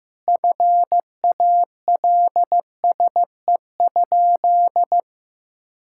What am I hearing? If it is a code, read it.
Morse code, heard as a keyed tone, about 15 wpm, 700 Hz. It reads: FALSE?
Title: Sentences from Top 1600 Words 15wpm